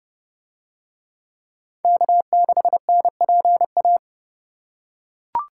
Morse code, heard as a keyed tone, about 28 wpm, 700 Hz. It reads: K6NPA